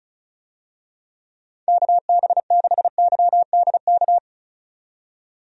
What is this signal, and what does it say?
Morse code, heard as a keyed tone, about 34 wpm, 700 Hz. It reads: KB6YDK
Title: USA Call Signs - 6 Characters - 35wpm